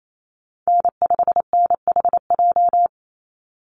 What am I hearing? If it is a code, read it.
Morse code, heard as a keyed tone, about 28 wpm, 700 Hz. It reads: N5NHJ
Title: Call Signs - 28wpm